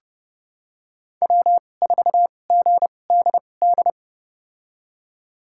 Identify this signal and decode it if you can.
Morse code, heard as a keyed tone, about 30 wpm, spaced as for 22 wpm, 700 Hz. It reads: W4GDD
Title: USA Call Signs - 5 Characters - 2x Character Spacing - 30wpm